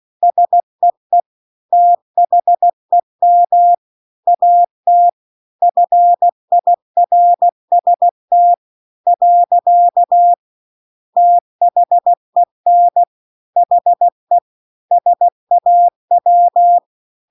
Morse code, heard as a keyed tone, about 16 wpm, 700 Hz. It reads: SEE THEM AT FIRST . THEN HE SAW
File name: CW-The-Big-Twoheaded-River-1-16WPM